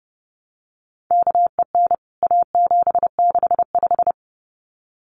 Morse code, heard as a keyed tone, about 30 wpm, 700 Hz. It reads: KEN A765
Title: CWT Exchange 30wpm